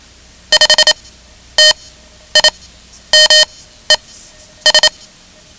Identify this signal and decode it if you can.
Morse code, heard as a keyed tone, about 27 wpm, spaced as for 11 wpm, 3050 Hz. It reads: 5 TIMES
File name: ham radio / CodeQuick / Code Quick 10